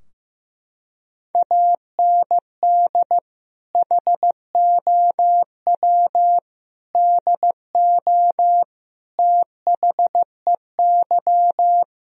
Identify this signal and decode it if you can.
Morse code, heard as a keyed tone, about 15 wpm, 700 Hz. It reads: AND HOW DO THEY
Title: Sentences from Top 1000 Words - Review Part 4 - 15wpm